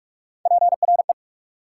Morse code, heard as a keyed tone, about 60 wpm, spaced as for 53 wpm, 690 Hz. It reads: PRE
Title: Speed-Racing Prefixes and Suffixes 45wpm